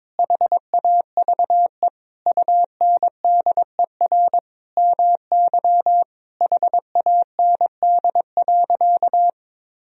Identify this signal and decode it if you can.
Morse code, heard as a keyed tone, about 22 wpm, 700 Hz. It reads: HAVE UNDER MY HAND.
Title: Sentences from Top 1000 Words - Review Part 1 - 22wpm